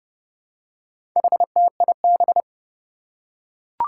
Morse code, heard as a keyed tone, about 30 wpm, 700 Hz. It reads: HTIB
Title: Four Challenging Letter-Number Combinations 30wpm